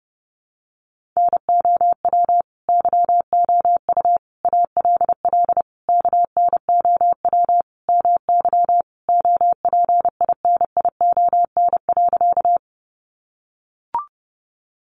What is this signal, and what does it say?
Morse code, heard as a keyed tone, about 28 wpm, 700 Hz. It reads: NOW YOU ALL KNOW MY OPINION.
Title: Sentences from Top 1200 Words 30wpm